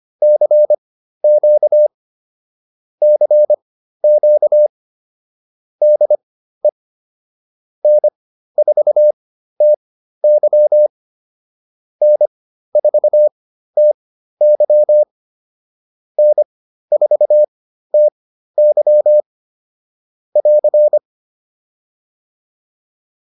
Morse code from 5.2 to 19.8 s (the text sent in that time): DE N4TY N4TY N4TY